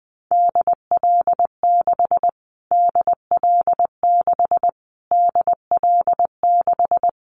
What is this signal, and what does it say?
Morse code, heard as a keyed tone, about 20 wpm, 700 Hz. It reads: DL6 DL6 DL6
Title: Callsign Prefixes for Top 25 DXCC - Repeated 3x - 20wpm